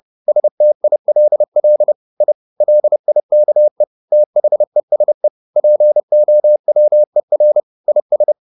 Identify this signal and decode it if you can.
Morse code, heard as a keyed tone, about 30 wpm, 600 Hz. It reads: STILL I LIKE THESE POWER IS